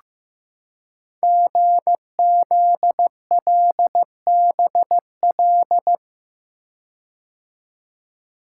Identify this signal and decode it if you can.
Morse code, heard as a keyed tone, about 15 wpm, 700 Hz. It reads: GZLBL
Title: Code Groups - 5 Letters - 15wpm